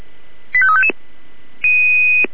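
Five-tone selective call (ZVEI), heard as quick runs of five tones, tones of about 70 ms. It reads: 85355
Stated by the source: ZVEI five-tone selective call received via SDR.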